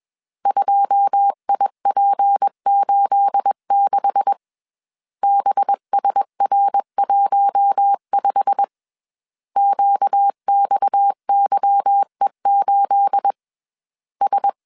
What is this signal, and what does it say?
Morse code, heard as a keyed tone, about 21 wpm, 800 Hz. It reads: 2IP86 BSR15 QXYE8 S